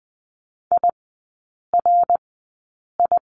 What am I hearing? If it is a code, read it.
Morse code, heard as a keyed tone, about 20 wpm, spaced as for 8 wpm, 700 Hz. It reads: IRI
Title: 053 Lesson 10e - Letter r - Three Characters 20wpm